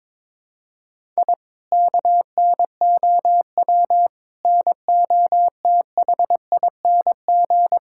Morse code, heard as a keyed tone, about 22 wpm, 700 Hz. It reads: I KNOW NOTHING
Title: Sentences from Top 1000 Words - Review Part 6 - 22wpm